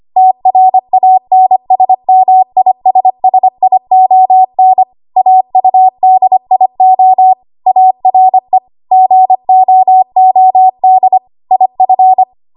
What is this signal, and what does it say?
Morse code, heard as a keyed tone, about 25 wpm, 750 Hz. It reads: TRANSMISSION AUDIO ARE GOOD IF
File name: Archive / 25 WPM / 25WPM